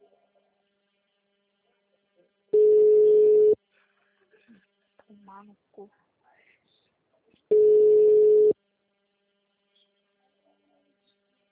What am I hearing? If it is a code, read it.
Ringback tone (Europe, 425 Hz), with speech.